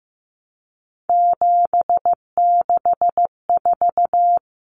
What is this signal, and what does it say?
Morse code, heard as a keyed tone, about 15 wpm, 700 Hz. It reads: 764
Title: Numbers - 3-Digit 15wpm